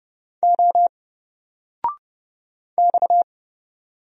Morse code, heard as a keyed tone, about 30 wpm, 700 Hz. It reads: O X
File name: Single-Letter-Number-30wpm